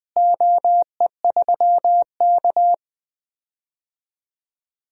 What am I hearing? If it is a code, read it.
Morse code, heard as a keyed tone, about 21 wpm, 700 Hz. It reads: OE3K
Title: DXCC-100 Call Signs - 4 Characters - Repeated 3x - 20wpm